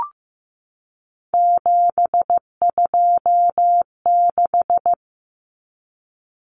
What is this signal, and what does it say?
Morse code, heard as a keyed tone, about 15 wpm, 700 Hz. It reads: 726
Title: Numbers - 3-Digit 15wpm